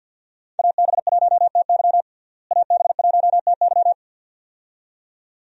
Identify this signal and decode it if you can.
Morse code, heard as a keyed tone, about 47 wpm, 700 Hz. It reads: AB1TY AB1TY
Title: USA Call Signs - 5 Characters - Repeated 2x - 50wpm